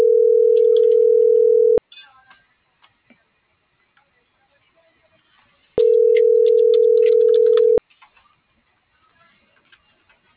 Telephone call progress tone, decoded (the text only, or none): ringback tone